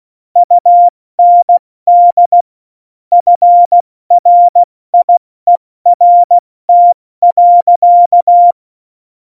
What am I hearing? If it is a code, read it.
Morse code, heard as a keyed tone, about 16 wpm, 700 Hz. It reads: UND FRIERT.
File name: Morsekurs / Witze / Witz 07. Januar 2020, 70 F80